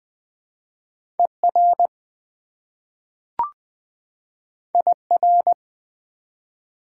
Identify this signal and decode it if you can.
Morse code, heard as a keyed tone, about 20 wpm, 700 Hz. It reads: ER IR
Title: Prefixes and Suffixes 20wpm